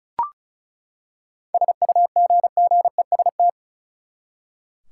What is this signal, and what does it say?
Morse code, heard as a keyed tone, about 36 wpm, 700 Hz. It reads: SUGGEST
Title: Seven Letter Words - 35wpm